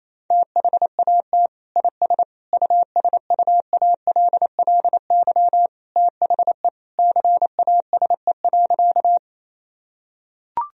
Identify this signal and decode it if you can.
Morse code, heard as a keyed tone, about 28 wpm, 700 Hz. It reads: THAT IS USUALLY THE CASE.
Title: Sentences from Top 1000 Words - Review Part 3 - 28wpm